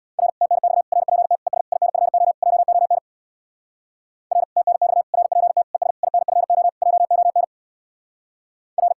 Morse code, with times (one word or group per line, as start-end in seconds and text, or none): 0.2-0.3 R
0.4-0.8 TNX
0.9-1.4 RPT
1.5-1.6 ES
1.7-2.3 INFO
2.4-3.0 JON
4.3-4.4 R
4.6-5.0 TNX
5.1-5.6 RPT
5.7-5.9 ES
6.0-6.7 INFO
6.8-7.4 JON
8.8-8.9 R